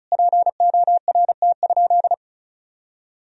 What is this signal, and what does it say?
Morse code, heard as a keyed tone, about 36 wpm, 700 Hz. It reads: PORT?
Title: Sentences from Top 1000 Words - Review Part 1 - 35wpm